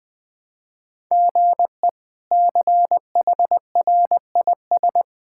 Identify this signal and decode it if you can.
Morse code, heard as a keyed tone, about 20 wpm, 700 Hz. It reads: GE CHRIS
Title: K1USN SST QSO Practice - 20wpm